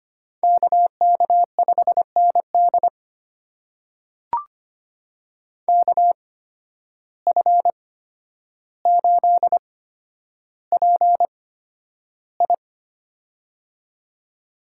Morse code, heard as a keyed tone, about 25 wpm, spaced as for 11 wpm, 700 Hz. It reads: KK5ND KF8PI
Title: USA Call Signs - 5 Characters - 8x Character Spacing - 25wpm